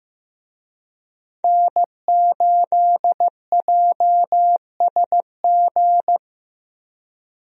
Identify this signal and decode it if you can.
Morse code, heard as a keyed tone, about 15 wpm, 700 Hz. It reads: N8JSG
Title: USA Call Signs - 5 Characters - 15wpm